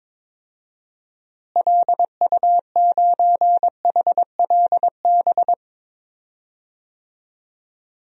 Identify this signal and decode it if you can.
Morse code, heard as a keyed tone, about 22 wpm, 700 Hz. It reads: LU9HLB
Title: DXCC-100 Call Signs - 6 Characters - Repeated 2x - 22wpm